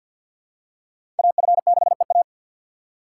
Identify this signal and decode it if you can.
Morse code, heard as a keyed tone, about 51 wpm, 700 Hz. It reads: AF6EA
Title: USA Call Signs - 5 Characters - 6x Character Spacing - 50wpm